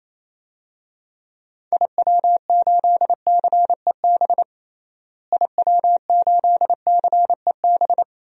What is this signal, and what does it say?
Morse code, heard as a keyed tone, about 28 wpm, 700 Hz. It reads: IW8CEB IW8CEB
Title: DXCC-100 Call Signs - 6 Characters - Repeated 2x - 28wpm